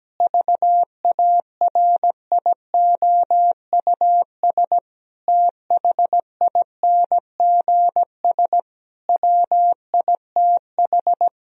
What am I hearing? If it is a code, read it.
Morse code, heard as a keyed tone, about 17 wpm, 700 Hz. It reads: VARIOUS THINGS WITH
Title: Sentences from Top 1000 Words - Review Part 4 - 17wpm